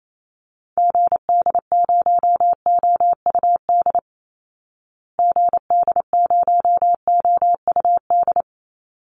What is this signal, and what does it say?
Morse code, heard as a keyed tone, about 28 wpm, 700 Hz. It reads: GD0OUD GD0OUD
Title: Call Signs Repeated 3x - 28wpm